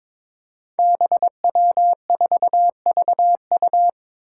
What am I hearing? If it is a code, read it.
Morse code, heard as a keyed tone, about 22 wpm, 700 Hz. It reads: BW4VU
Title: Five Challenging Letter-Number Combinations 22wpm